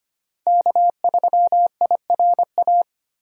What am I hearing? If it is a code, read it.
Morse code, heard as a keyed tone, about 25 wpm, 690 Hz. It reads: K3IRA
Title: USA Call Signs - 5 Characters - 25wpm